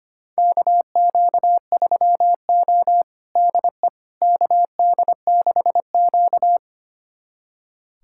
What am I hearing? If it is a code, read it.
Morse code, heard as a keyed tone, about 25 wpm, 700 Hz. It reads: KQ3O DE KD6Q
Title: USA Callsign DE Callsign - 4 Characters - 25wpm